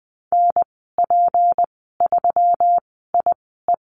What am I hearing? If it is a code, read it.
Morse code, heard as a keyed tone, about 19 wpm, spaced as for 14 wpm, 700 Hz. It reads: NP3IE
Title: 186 Lesson 30g - Letter b - Call Signs 20wpm